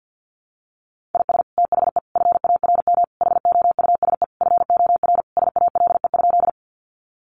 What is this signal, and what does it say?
Morse code, heard as a keyed tone, about 48 wpm, 700 Hz. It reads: IS THE FARM HOUSE FOR SALE?
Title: Sentences from Top 1400 Words 50wpm